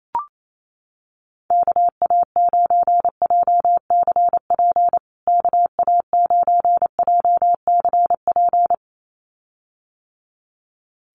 Morse code, heard as a keyed tone, about 28 wpm, 700 Hz. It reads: KA9JCP KA9JCP
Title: USA Call Signs Repeated 2x 28wpm